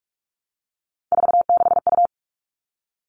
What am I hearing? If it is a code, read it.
Morse code, heard as a keyed tone, about 44 wpm, 700 Hz. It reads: 46U
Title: All Cut Numbers - 3-Digit 45wpm